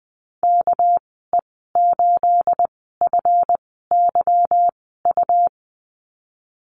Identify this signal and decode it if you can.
Morse code, heard as a keyed tone, about 20 wpm, spaced as for 14 wpm, 700 Hz. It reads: KE8FYU
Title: 219 Lesson 35g - Number 0 - Call Signs 20wpm